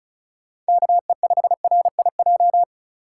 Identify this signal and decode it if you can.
Morse code, heard as a keyed tone, about 34 wpm, 700 Hz. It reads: KE5RIJ
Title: USA Call Signs - 6 Characters - 3x Character Spacing - 35wpm